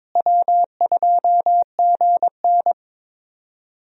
Morse code, heard as a keyed tone, about 22 wpm, 700 Hz. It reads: W2GN
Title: USA Call Signs - 4 Characters - Repeated 2x - 22wpm